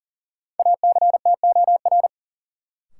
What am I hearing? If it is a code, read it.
Morse code, heard as a keyed tone, about 40 wpm, 700 Hz. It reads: ACTOR
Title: Sentences from Top 1500 Words - New Word Drill Alt - 40wpm